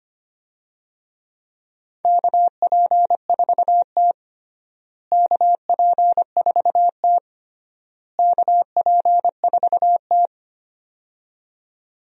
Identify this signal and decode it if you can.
Morse code, heard as a keyed tone, about 25 wpm, 700 Hz. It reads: KP4T KP4T KP4T
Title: DXCC-100 Call Signs - 4 Characters - Repeated 3x - 25wpm